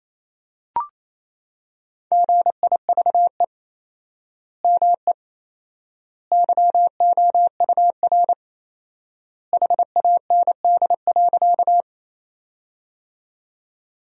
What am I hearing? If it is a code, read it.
Morse code, heard as a keyed tone, about 28 wpm, 700 Hz. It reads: GIVE ME YOUR HAND.
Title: 4x Word-Spacing - Sentences from Top 200 Words - 28wpm